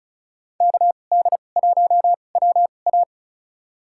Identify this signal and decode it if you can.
Morse code, heard as a keyed tone, about 36 wpm, spaced as for 25 wpm, 700 Hz. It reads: KD1WA
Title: USA Call Signs - 5 Characters - 2x Character Spacing - 35wpm